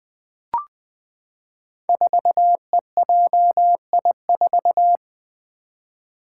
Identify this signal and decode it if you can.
Morse code, heard as a keyed tone, about 20 wpm, 700 Hz. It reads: 4EJI4
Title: Five Challenging Letter-Number Combinations 20wpm